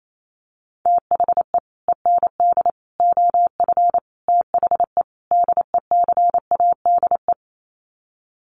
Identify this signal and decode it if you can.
Morse code, heard as a keyed tone, about 28 wpm, 700 Hz. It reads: THE END OF THE DECADE
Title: Sets of 5 Words Encore 28wpm